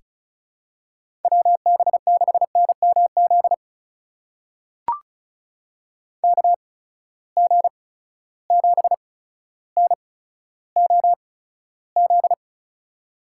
Morse code, heard as a keyed tone, about 36 wpm, spaced as for 15 wpm, 700 Hz. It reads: WB6NMZ KG7NOZ
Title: USA Call Signs - 6 Characters - 8x Character Spacing - 35wpm